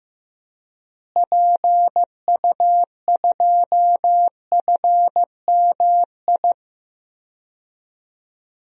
Morse code, heard as a keyed tone, about 15 wpm, 700 Hz. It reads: PU2FMI
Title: DXCC-100 Call Signs - 6 Characters - Repeated 3x - 15wpm